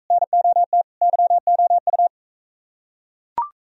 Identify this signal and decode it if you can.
Morse code, heard as a keyed tone, about 42 wpm, 700 Hz. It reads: NOT YOU